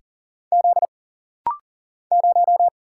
Morse code, heard as a keyed tone, about 40 wpm, 700 Hz. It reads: Z 0
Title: Single Letter-Number - Rapid-Fire 40wpm